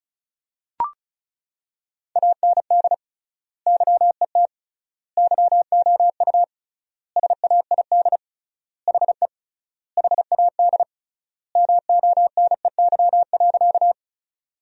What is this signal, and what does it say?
Morse code, heard as a keyed tone, about 34 wpm, 700 Hz. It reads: AND YET YOU SAID HE HAD MONEY.
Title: 3x Word-Spacing - Sentences from Top 500 Words - 35wpm